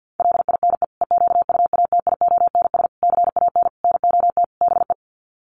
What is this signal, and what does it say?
Morse code, heard as a keyed tone, about 50 wpm, 700 Hz. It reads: LINE EQUATIONS CAN NOT BE
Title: Sets of 5 Words 50wpm